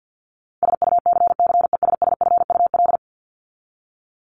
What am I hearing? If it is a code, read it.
Morse code, heard as a keyed tone, about 48 wpm, 700 Hz. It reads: SUCCESSFUL
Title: Top 751-1000 Words 50wpm